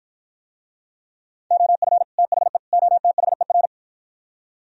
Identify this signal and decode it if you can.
Morse code, heard as a keyed tone, about 52 wpm, 700 Hz. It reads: OF THE OTHER